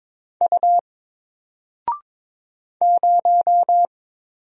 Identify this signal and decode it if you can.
Morse code, heard as a keyed tone, about 22 wpm, 700 Hz. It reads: U 0
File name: Single-Letter-Number-22wpm